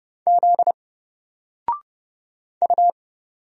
Morse code, heard as a keyed tone, about 31 wpm, 700 Hz. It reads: Z U
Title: Single Letter-Number 30wpm